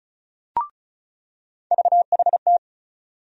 Morse code, heard as a keyed tone, about 34 wpm, 700 Hz. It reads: VHT
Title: USA Call Sign Postfixes - 3 Letters 35wpm